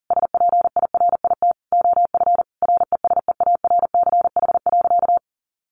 Morse code, heard as a keyed tone, about 40 wpm, 700 Hz. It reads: SPIRIT OF RESEARCH.
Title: Sentences from Top 1400 Words 40wpm